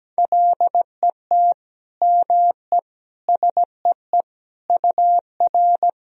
Morse code, heard as a keyed tone, about 17 wpm, 700 Hz. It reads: LET ME SEE UR
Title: Sentences from Top 500 Words and Abbreviations 17wpm